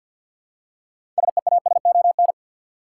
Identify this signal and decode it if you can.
Morse code, heard as a keyed tone, about 50 wpm, 700 Hz. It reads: SEASON
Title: Six Letter Words - 50wpm